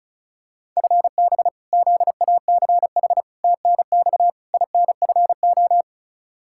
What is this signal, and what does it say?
Morse code, heard as a keyed tone, about 34 wpm, 700 Hz. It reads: FB ZACH TNX INFO
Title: K1USN SST QSO Practice 35wpm